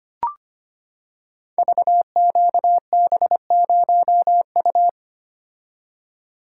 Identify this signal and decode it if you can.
Morse code, heard as a keyed tone, about 25 wpm, 700 Hz. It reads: VQB0U
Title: Code Groups - 5 Letter:Numbers - 25wpm